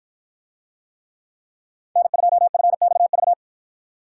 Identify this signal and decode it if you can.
Morse code, heard as a keyed tone, about 48 wpm, 700 Hz. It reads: N2FXV